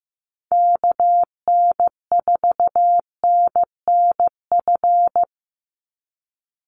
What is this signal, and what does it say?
Morse code, heard as a keyed tone, about 15 wpm, 700 Hz. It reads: KN4NNF
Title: USA Call Signs 15wpm